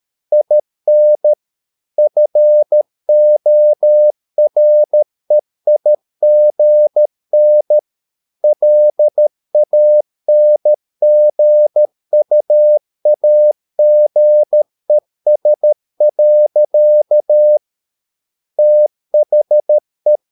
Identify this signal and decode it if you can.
Morse code, heard as a keyed tone, about 13 wpm, 600 Hz. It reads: IN FOREIGN LANGUAGES. THE